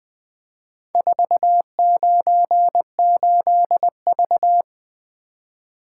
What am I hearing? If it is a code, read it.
Morse code, heard as a keyed tone, about 20 wpm, 700 Hz. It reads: 498V
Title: Four Letter-Number Combinations - 20wpm